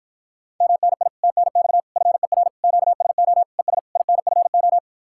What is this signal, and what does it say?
Morse code, heard as a keyed tone, about 52 wpm, 700 Hz. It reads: MNI TNX FER QSO ES INFO